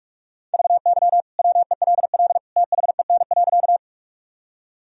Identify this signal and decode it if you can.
Morse code, heard as a keyed tone, about 41 wpm, 700 Hz. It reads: VY WELL THEN.
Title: Sentences from Top 200 Words and Abbreviations 45wpm